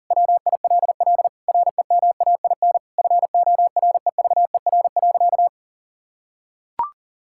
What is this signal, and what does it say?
Morse code, heard as a keyed tone, about 40 wpm, 700 Hz. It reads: WILL REMAIN FOREVER.
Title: Sentences from Top 1600 Words 40wpm